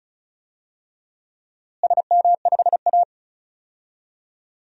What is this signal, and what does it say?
Morse code, heard as a keyed tone, about 36 wpm, 700 Hz. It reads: SM5A
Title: DXCC-100 Call Signs - 4 Characters - Repeated 3x - 35wpm